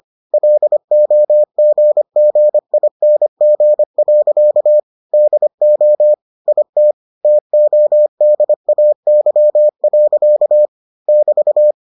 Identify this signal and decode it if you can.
Morse code, heard as a keyed tone, about 25 wpm, 600 Hz. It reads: LOGGING. DO IT TODAY. <BT>